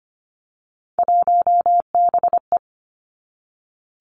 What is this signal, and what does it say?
Morse code, heard as a keyed tone, about 25 wpm, 700 Hz. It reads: 1BE
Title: Three Challenging Letter-Number Combinations 25wpm